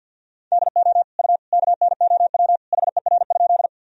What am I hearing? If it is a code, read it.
Morse code, heard as a keyed tone, about 52 wpm, 700 Hz. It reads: DO U KNOW HER?
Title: Sentences from Top 100 Words and Abbreviations 50wpm